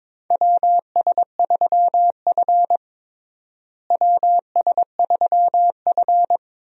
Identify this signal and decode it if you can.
Morse code, heard as a keyed tone, about 21 wpm, 700 Hz. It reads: WS3F WS3F
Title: USA Call Signs - 4 Characters - Repeated 2x - 22wpm